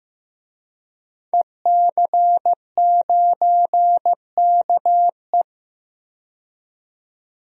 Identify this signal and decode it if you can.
Morse code, heard as a keyed tone, about 15 wpm, 700 Hz. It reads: EC9KE
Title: DXCC-100 Call Signs - 5 Characters - 15wpm